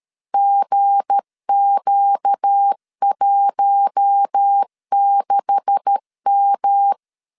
Morse code, heard as a keyed tone, about 13 wpm, 800 Hz. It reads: GQ16M
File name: CW / DET / 1 9 1500C S C 12wpm 3811